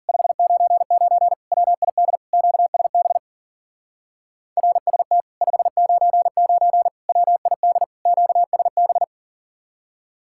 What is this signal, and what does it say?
Morse code, heard as a keyed tone, about 48 wpm, 700 Hz. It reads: 599 WID QSB RST 599 WID QSB
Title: Speed-Racing Common QSO Phrases Encore - 40wpm